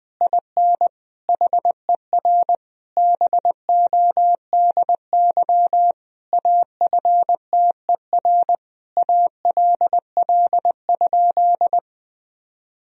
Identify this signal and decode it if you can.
Morse code, heard as a keyed tone, about 20 wpm, 700 Hz. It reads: IN HER BODY AFTER ALL?
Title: Sentences from Top 1000 Words - Review Part 2 - 20wpm